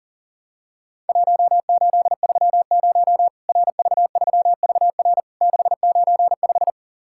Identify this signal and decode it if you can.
Morse code, heard as a keyed tone, about 37 wpm, 700 Hz. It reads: 1830 RV3VR 695
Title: Worked All Europe DX Contest - QTC Exchange - 40wpm